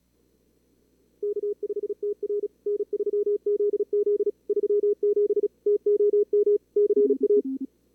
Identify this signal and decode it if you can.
Morse code, heard as a keyed tone, about 39 wpm, 440 Hz. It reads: K5TR N3ZZ 37 TOM CA
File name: CWSprint / Feb2022 / k5tr-feb-na-sprint-cw-k5ot -2300